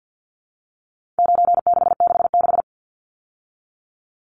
Morse code, heard as a keyed tone, about 50 wpm, 700 Hz. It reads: 9666
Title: Numbers - 4-Digit 50wpm